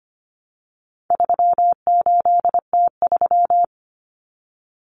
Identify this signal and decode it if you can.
Morse code, heard as a keyed tone, about 25 wpm, 700 Hz. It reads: 38T3
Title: Common Cut Numbers - 4-Digit 25wpm